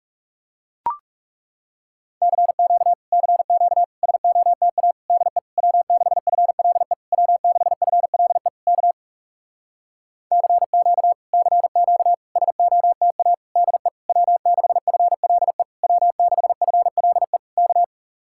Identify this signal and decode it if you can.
Morse code, heard as a keyed tone, about 39 wpm, 700 Hz. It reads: CQ CQ SOTA DE W6FLE W6FLE K CQ CQ SOTA DE W6FLE W6FLE K